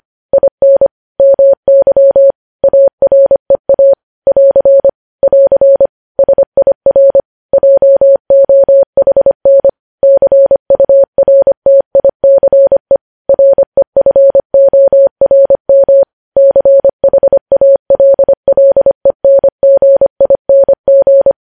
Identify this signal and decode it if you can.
Morse code, heard as a keyed tone, about 25 wpm, 550 Hz. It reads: IN MY AREA <AR> <AR> SIR JOHN CURTICE REFORM CHALLENGING
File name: podcast / News headlines-1x-25x25